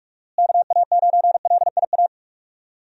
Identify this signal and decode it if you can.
Morse code, heard as a keyed tone, about 43 wpm, 700 Hz. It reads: KA9LIA
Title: USA Call Signs - 6 Characters - Repeated 2x - 45wpm